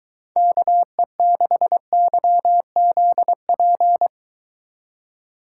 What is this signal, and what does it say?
Morse code, heard as a keyed tone, about 23 wpm, 700 Hz. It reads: KE6YZP